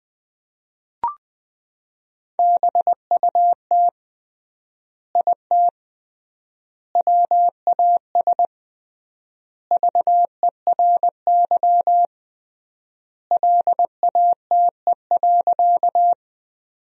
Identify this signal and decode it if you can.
Morse code, heard as a keyed tone, about 21 wpm, 700 Hz. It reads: BUT IT WAS VERY LATE.
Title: 3x Word-Spacing - Sentences from Top 300 Words - 20wpm